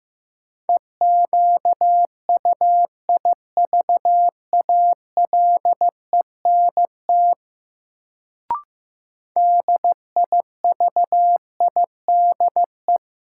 Morse code, heard as a keyed tone, about 15 wpm, 700 Hz. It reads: EQUIVALENT DIVIDE
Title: Sentences from Top 1600 Words - New Word Drill - 15wpm